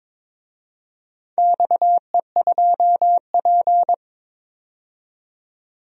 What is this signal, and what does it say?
Morse code, heard as a keyed tone, about 22 wpm, 700 Hz. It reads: XE2P
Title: DXCC-100 Call Signs - 4 Characters - 22wpm